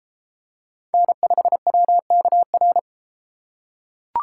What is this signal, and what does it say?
Morse code, heard as a keyed tone, about 31 wpm, 700 Hz. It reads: N5WKR